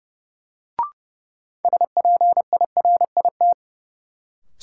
Morse code, heard as a keyed tone, about 30 wpm, 700 Hz. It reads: SPIRIT
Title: Sentences from Top 1400 Words - New Word Drill Alt - 30wpm